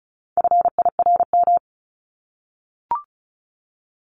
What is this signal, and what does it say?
Morse code, heard as a keyed tone, about 34 wpm, 700 Hz. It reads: FIRM
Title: Top 1251-1500 Words 35wpm